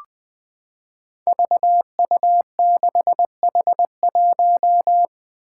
Morse code, heard as a keyed tone, about 20 wpm, 700 Hz. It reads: VU6H1
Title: Five Challenging Letter-Number Combinations 20wpm